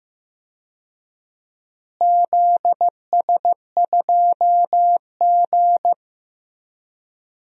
Morse code, heard as a keyed tone, about 15 wpm, 700 Hz. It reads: ZS2G
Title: DXCC-100 Call Signs - 4 Characters - Repeated 3x - 15wpm